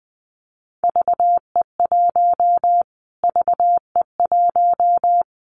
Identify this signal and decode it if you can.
Morse code, heard as a keyed tone, about 20 wpm, 700 Hz. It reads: VE1 VE1
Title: Callsign Prefixes for Top 25 DXCC - Repeated 3x - 20wpm